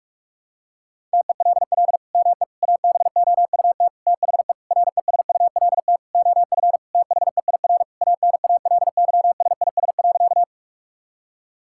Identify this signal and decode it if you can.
Morse code, heard as a keyed tone, about 46 wpm, 700 Hz. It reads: TELL ME ABOUT THE RESULT OF THEIR ANALYSIS.
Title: Sentences from Top 1000 Words - Review Part 5 - 45wpm